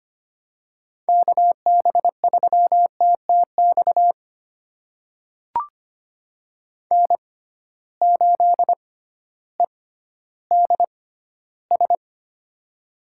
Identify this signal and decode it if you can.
Morse code, heard as a keyed tone, about 25 wpm, spaced as for 13 wpm, 700 Hz. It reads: KB3TTX N8EDS
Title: USA Call Signs - 6x Character Spacing - 25wpm